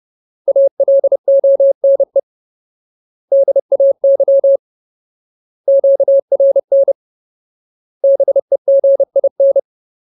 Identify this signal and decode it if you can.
Morse code, heard as a keyed tone, about 30 wpm, 550 Hz. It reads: ALONE DAY QRN BEGIN
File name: Ultimate Word List / 30WPM Ultimate